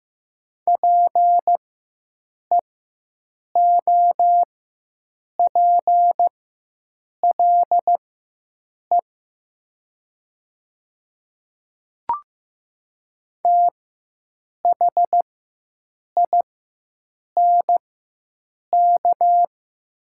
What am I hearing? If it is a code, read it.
Morse code, heard as a keyed tone, about 15 wpm, spaced as for 7 wpm, 700 Hz. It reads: PEOPLE THINK
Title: 4x Character Spacing - Top 200 Words - 15wpm